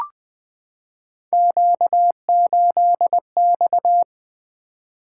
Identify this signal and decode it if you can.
Morse code, heard as a keyed tone, about 20 wpm, 700 Hz. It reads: Q8X
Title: Three Letter-Number Combinations - 20wpm